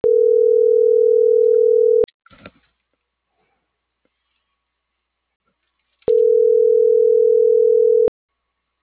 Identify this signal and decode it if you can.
Ringback tone (North America), 440 + 480 Hz.